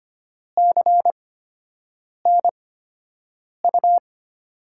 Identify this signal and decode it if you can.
Morse code, heard as a keyed tone, about 25 wpm, spaced as for 7 wpm, 700 Hz. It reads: CNU
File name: Top 90 QSO Elements - 8x Character Spacing - 25wpm